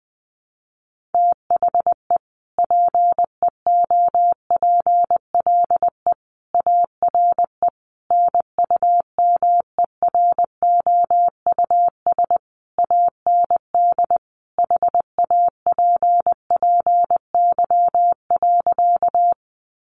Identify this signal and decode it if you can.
Morse code, heard as a keyed tone, about 20 wpm, 700 Hz. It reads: THE PEOPLE ARE NUMEROUS AND HAPPY.
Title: Sentences from Top 1400 Words 20wpm